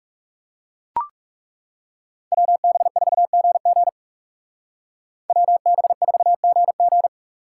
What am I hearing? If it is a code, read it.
Morse code, heard as a keyed tone, about 37 wpm, 700 Hz. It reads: WB4GG WB4GG
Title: Speed Racing - USA 2x2 Call Signs 30wpm